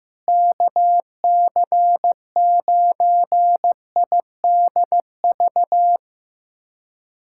Morse code, heard as a keyed tone, about 15 wpm, 700 Hz. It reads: KC9IDV
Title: USA Callsign DE Callsign - 6 Characters - 15wpm